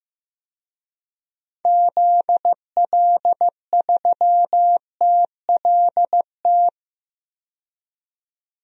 Morse code, heard as a keyed tone, about 15 wpm, 700 Hz. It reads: ZL3TLT
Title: DXCC-100 Call Signs - 6 Characters - 15wpm